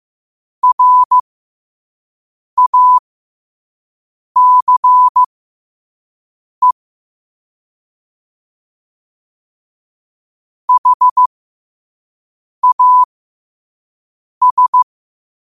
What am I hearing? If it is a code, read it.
Morse code, heard as a keyed tone, about 15 wpm, spaced as for 5 wpm, 1000 Hz. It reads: RACE HAS
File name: qotdfiles / 5wpm